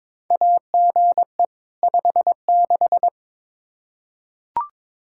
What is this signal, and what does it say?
Morse code, heard as a keyed tone, about 21 wpm, 700 Hz. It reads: AGE 56
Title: K1USN SST QSO Practice 22wpm